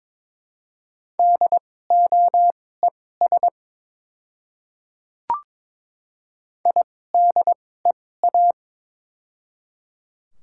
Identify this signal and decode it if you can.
Morse code, heard as a keyed tone, about 22 wpm, spaced as for 16 wpm, 700 Hz. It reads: DOES IDEA
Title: 2x Character Spacing - Top 500 Words - 22wpm